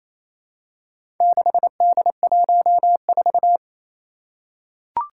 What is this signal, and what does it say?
Morse code, heard as a keyed tone, about 28 wpm, 700 Hz. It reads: 6D14
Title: Four Challenging Letter-Number Combinations 28wpm